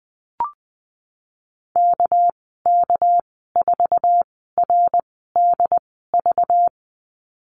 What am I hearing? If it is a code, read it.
Morse code, heard as a keyed tone, about 20 wpm, spaced as for 15 wpm, 700 Hz. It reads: KK4RDV
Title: 213 Lesson 34g - Number 8 - Call Signs 20wpm